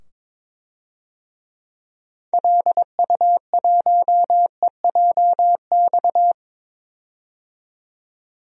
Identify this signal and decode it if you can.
Morse code, heard as a keyed tone, about 22 wpm, 700 Hz. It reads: LU1EJX
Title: DXCC-100 Call Signs - 6 Characters - 22wpm